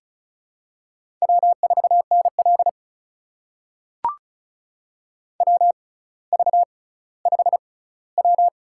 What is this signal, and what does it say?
Morse code, heard as a keyed tone, about 34 wpm, spaced as for 18 wpm, 700 Hz. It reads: W4NL WV5W
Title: USA Call Signs - 4 Characters - 6x Character Spacing - 35wpm